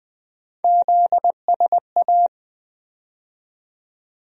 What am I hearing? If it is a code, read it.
Morse code, heard as a keyed tone, about 20 wpm, 700 Hz. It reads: ZSA
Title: USA Call Sign Postfixes - 3 Letters 20wpm